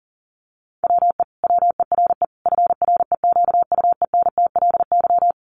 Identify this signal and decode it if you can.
Morse code, heard as a keyed tone, about 40 wpm, 700 Hz. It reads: WE WERE FREQUENTLY
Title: Sentences from Top 1300 Words 40wpm